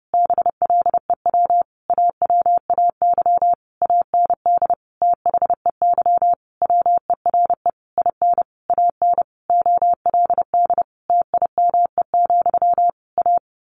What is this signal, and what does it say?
Morse code, heard as a keyed tone, about 30 wpm, 700 Hz. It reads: BLEW AWAY AND THEY WERE IN AN OLD TIME, A
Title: A Sound Of Thunder 30wpm